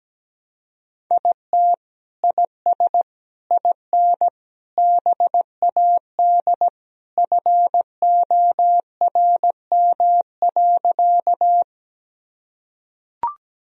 Morse code, heard as a keyed tone, about 17 wpm, 700 Hz. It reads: IT IS IN BAD FORM.
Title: Sentences from Top 1000 Words - Review Part 1 - 17wpm